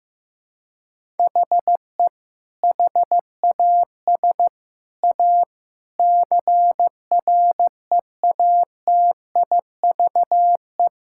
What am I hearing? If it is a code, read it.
Morse code, heard as a keyed tone, about 16 wpm, 700 Hz. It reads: HE HAS A CREATIVE
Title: Sentences from Top 1000 Words - Review Part 5 - 15wpm